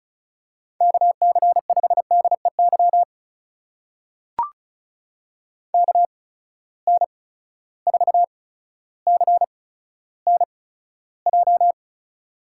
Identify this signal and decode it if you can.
Morse code, heard as a keyed tone, about 36 wpm, spaced as for 15 wpm, 700 Hz. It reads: KC5DEY KN4CNJ
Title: USA Call Signs - 8x Character Spacing - 35wpm